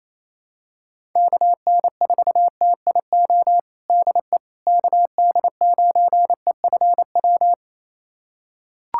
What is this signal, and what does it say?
Morse code, heard as a keyed tone, about 28 wpm, 700 Hz. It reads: KN4TIO DE KD9EFW
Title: USA Callsign DE Callsign - 6 Characters - 28wpm